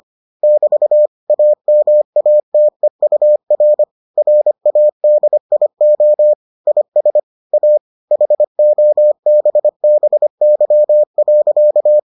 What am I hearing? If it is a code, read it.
Morse code, heard as a keyed tone, about 25 wpm, 600 Hz. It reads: <BT> AMATEUR RADIO IS A HOBBY.